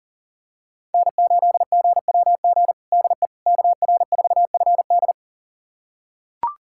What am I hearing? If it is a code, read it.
Morse code, heard as a keyed tone, about 37 wpm, 700 Hz. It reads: N8GWG DE KR4FD
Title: USA Callsign DE Callsign - 5 Characters - 40wpm